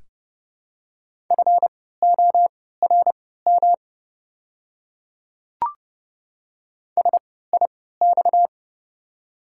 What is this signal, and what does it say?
Morse code, heard as a keyed tone, about 29 wpm, spaced as for 17 wpm, 700 Hz. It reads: FORM SIX
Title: 3x Character Spacing - Top 500 Words - 30wpm